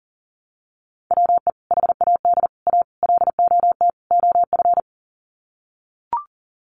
Morse code, heard as a keyed tone, about 40 wpm, 700 Hz. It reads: WE HAD A LOT OF
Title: Sets of 5 Words Encore 40wpm